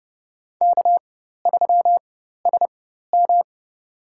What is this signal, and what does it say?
Morse code, heard as a keyed tone, about 30 wpm, spaced as for 14 wpm, 700 Hz. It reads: K3SM
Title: USA Call Signs - 4 Characters - 4x Character Spacing - 30wpm